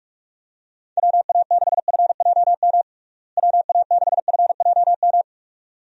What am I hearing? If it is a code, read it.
Morse code, heard as a keyed tone, about 42 wpm, 700 Hz. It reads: WA6FJM WA6FJM
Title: USA Call Signs - 6 Characters - Repeated 2x - 45wpm